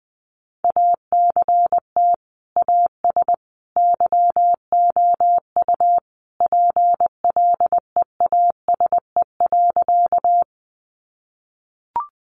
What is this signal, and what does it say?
Morse code, heard as a keyed tone, about 20 wpm, 700 Hz. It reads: ACT AS YOU PLEASE.
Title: Sentences from Top 1500 Words 20wpm